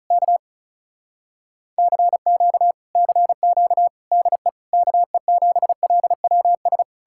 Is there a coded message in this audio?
Yes: Morse code from 0.1 to 6.8 s, heard as a keyed tone, about 46 wpm, 700 Hz. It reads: K CQ CQ DE KE7LWS